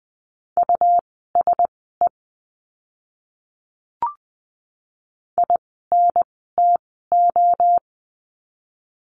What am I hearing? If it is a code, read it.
Morse code, heard as a keyed tone, about 20 wpm, spaced as for 14 wpm, 700 Hz. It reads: USE INTO
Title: 206 Lesson 33f - Letter j - Words 20wpm